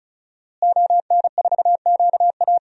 Morse code, heard as a keyed tone, about 36 wpm, 700 Hz. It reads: ON4QA
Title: DXCC-100 Call Signs - 5 Characters - Repeated 2x - 35wpm